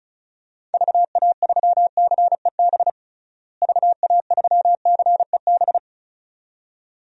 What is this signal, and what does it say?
Morse code, heard as a keyed tone, about 36 wpm, 700 Hz. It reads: VA3CEB VA3CEB
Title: DXCC-100 Call Signs - 6 Characters - Repeated 2x - 35wpm